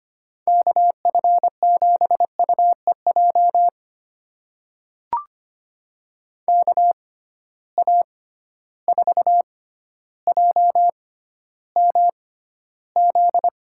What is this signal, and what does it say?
Morse code, heard as a keyed tone, about 25 wpm, spaced as for 13 wpm, 700 Hz. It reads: KF7UEJ KA4JMZ
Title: USA Call Signs - 6 Characters - 6x Character Spacing - 25wpm